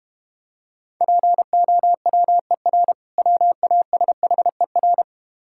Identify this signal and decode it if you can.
Morse code, heard as a keyed tone, about 32 wpm, 700 Hz. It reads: POWER WASHER